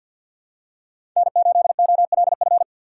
Morse code, heard as a keyed tone, about 47 wpm, 700 Hz. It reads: N8OLF
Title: USA Call Signs - 4x Character Spacing - 50wpm